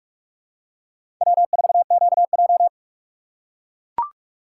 Morse code, heard as a keyed tone, about 41 wpm, 700 Hz. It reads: W4QJ
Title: USA Call Signs - 2x Character Spacing - 45wpm